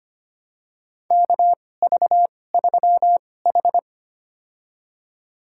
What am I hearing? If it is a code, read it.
Morse code, heard as a keyed tone, about 25 wpm, spaced as for 18 wpm, 700 Hz. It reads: KV3H
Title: USA Call Signs - 2x Character Spacing - 25wpm